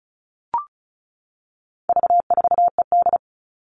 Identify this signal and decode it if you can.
Morse code, heard as a keyed tone, about 34 wpm, 700 Hz. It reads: V4ED
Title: All Cut Numbers - 4-Digit 35wpm